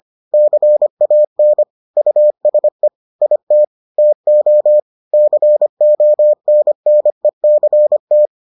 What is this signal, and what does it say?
Morse code, heard as a keyed tone, about 25 wpm, 600 Hz. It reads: CAN USE IT TO CONNECT